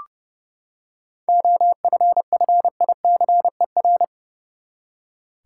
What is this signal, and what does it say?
Morse code, heard as a keyed tone, about 30 wpm, 700 Hz. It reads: OFFICER
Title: Seven Letter Words - 30wpm